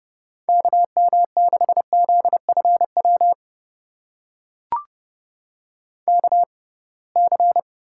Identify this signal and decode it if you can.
Morse code, heard as a keyed tone, about 30 wpm, spaced as for 23 wpm, 700 Hz. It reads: KM6ZFW KC
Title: USA Call Signs - 6 Characters - 6x Character Spacing - 30wpm